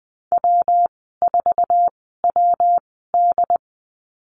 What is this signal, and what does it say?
Morse code, heard as a keyed tone, about 20 wpm, spaced as for 15 wpm, 700 Hz. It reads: W4WD
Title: 200 Lesson 32g - Letter k - Call Signs 20wpm